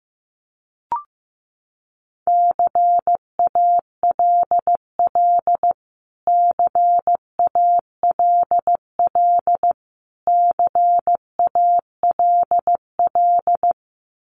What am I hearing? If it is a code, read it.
Morse code, heard as a keyed tone, about 15 wpm, 700 Hz. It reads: CALL CALL CALL
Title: Top 200 Words Repeated 15wpm